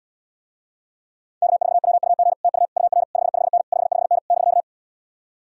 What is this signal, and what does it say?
Morse code, heard as a keyed tone, about 77 wpm, 700 Hz. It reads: K4WUG TU UR 55N 55N <BK>